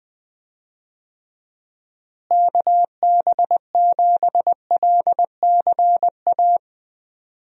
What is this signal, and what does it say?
Morse code, heard as a keyed tone, about 20 wpm, 700 Hz. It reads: KB7LCA
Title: USA Call Signs - 6 Characters - Repeated 2x - 20wpm